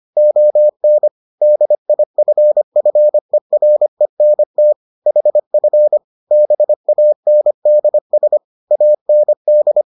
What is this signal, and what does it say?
Morse code, heard as a keyed tone, about 25 wpm, 600 Hz. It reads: ON DIFFERENT HF BANDS AND